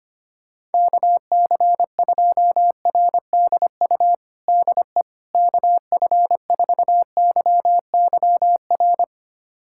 Morse code, heard as a keyed tone, about 25 wpm, 700 Hz. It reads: KC2RDU DE KF4YYR
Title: USA Callsign DE Callsign 25wpm